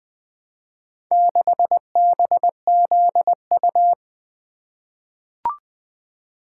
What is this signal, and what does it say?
Morse code, heard as a keyed tone, about 20 wpm, 700 Hz. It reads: 6BZU
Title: Four Letter-Number Combinations - 20wpm